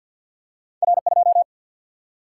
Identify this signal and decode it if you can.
Morse code, heard as a keyed tone, about 50 wpm, 700 Hz. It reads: R 2
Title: Single-Letter Single-Number 50wpm